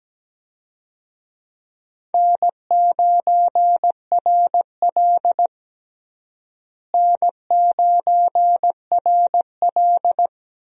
Morse code, heard as a keyed tone, about 17 wpm, 700 Hz. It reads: N9RL N9RL
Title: USA Call Signs - 4 Characters - Repeated 2x - 17wpm